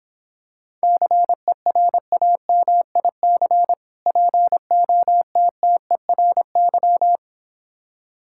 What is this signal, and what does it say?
Morse code, heard as a keyed tone, about 26 wpm, 700 Hz. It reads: CERAMIC POTTERY